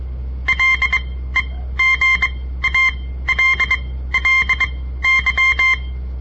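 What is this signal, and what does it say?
Morse code, heard as a keyed tone, about 23 wpm, spaced as for 15 wpm, 2100 Hz. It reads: LEGALLY
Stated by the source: On-air Morse code practice broadcast by a southern Montana amateur radio association; all code, no read back.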